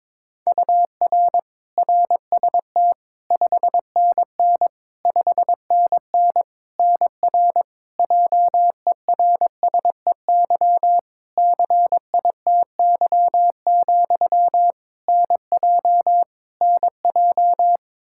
Morse code, heard as a keyed tone, about 22 wpm, 700 Hz. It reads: UR RST 5NN 5NN NR JERSEY CITY, NJ NJ
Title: Practice QSO - Exchange 1 - Beginning - 22wpm